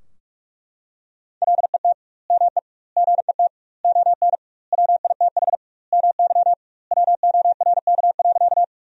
Morse code, heard as a keyed tone, about 43 wpm, 700 Hz. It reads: LET ME GET ON WITH MY WORK.
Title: 2x Word-Spacing - Sentences from Top 300 Words - 45wpm